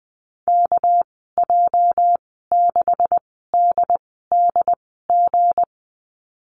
Morse code, heard as a keyed tone, about 20 wpm, spaced as for 15 wpm, 700 Hz. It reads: KJ6DDG
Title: 245 Lesson 39g - Letter z - Call Signs 20wpm